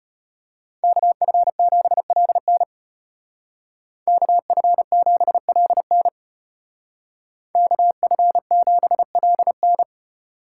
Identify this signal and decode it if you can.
Morse code, heard as a keyed tone, about 31 wpm, 700 Hz. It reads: KF7LN KF7LN KF7LN